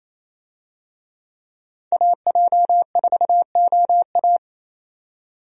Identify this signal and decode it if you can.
Morse code, heard as a keyed tone, about 28 wpm, 700 Hz. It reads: AJ4OA
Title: USA Call Signs - 5 Characters - 28wpm